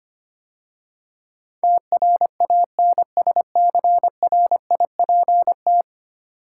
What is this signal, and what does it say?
Morse code, heard as a keyed tone, about 25 wpm, 700 Hz. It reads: TRANSCRIPT
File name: 10 Letter Words - First 3 Letters Given - 25wpm